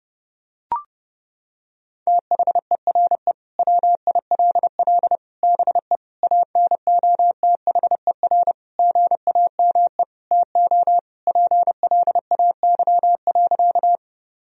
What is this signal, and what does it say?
Morse code, heard as a keyed tone, about 30 wpm, 700 Hz. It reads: THERE WILL BE ANOTHER GAME TO PLAY.
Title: Sentences from Top 1000 Words - Review Part 3 - 30wpm